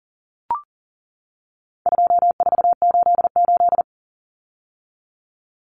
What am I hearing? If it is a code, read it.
Morse code, heard as a keyed tone, about 39 wpm, 700 Hz. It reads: 2488
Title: Numbers - 4-Digit 40wpm